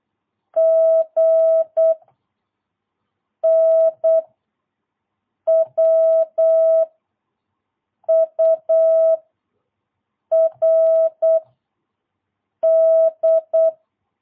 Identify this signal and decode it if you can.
Morse code, heard as a keyed tone, about 8 wpm, 660 Hz. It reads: G N W U R D
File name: jh8cbh / 2022cw